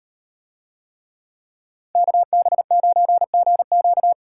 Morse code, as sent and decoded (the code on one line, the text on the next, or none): -.- -... ----. --. --.-
KB9GQ